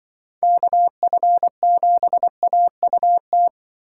Morse code, heard as a keyed tone, about 24 wpm, 700 Hz. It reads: KF7AUT
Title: Speed Racing - USA 2x3 Call Signs 20wpm